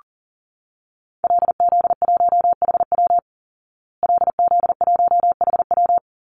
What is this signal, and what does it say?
Morse code, heard as a keyed tone, about 40 wpm, 700 Hz. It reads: LZ1HW LZ1HW
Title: Call Signs Repeated 3x - 40wpm